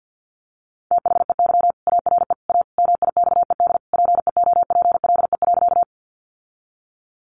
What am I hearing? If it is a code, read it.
Morse code, heard as a keyed tone, about 48 wpm, 700 Hz. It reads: THEY ARE A MIXED PEOPLE.
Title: Sentences from Top 1500 Words 50wpm